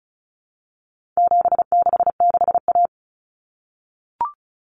Morse code, heard as a keyed tone, about 34 wpm, 700 Hz. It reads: 766A
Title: Common Cut Numbers - 4-Digit 35wpm